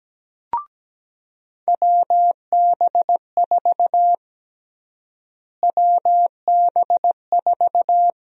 Morse code, heard as a keyed tone, about 17 wpm, 700 Hz. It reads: WB4 WB4
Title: USA Call Sign Prefixes - Repeated 2x - 17wpm